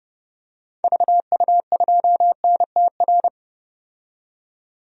Morse code, heard as a keyed tone, about 30 wpm, 700 Hz. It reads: VU2NTR
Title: DXCC-100 Call Signs - 6 Characters - 30wpm